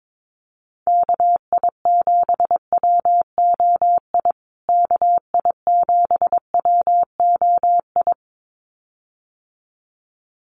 Morse code, heard as a keyed tone, about 22 wpm, 700 Hz. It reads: KI7WOI KI7WOI
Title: USA Call Signs Repeated 2x 22wpm